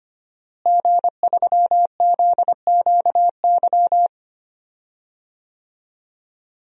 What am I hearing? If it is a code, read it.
Morse code, heard as a keyed tone, about 25 wpm, 700 Hz. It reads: G3ZQY
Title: DXCC-100 Call Signs - 5 Characters - Repeated 3x - 25wpm